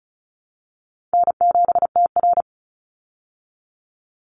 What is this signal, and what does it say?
Morse code, heard as a keyed tone, about 34 wpm, 700 Hz. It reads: N7TR
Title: Call Signs Repeated 2x - 35wpm